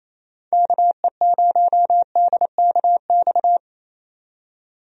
Morse code, heard as a keyed tone, about 27 wpm, 700 Hz. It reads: KE0DKX
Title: USA Call Signs - 6 Characters - 28wpm